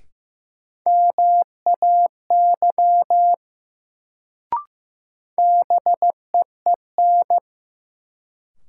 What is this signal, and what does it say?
Morse code, heard as a keyed tone, about 15 wpm, 700 Hz. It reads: MAY BEEN
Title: Sentences from Top 100 Words - New Word Drill - 15wpm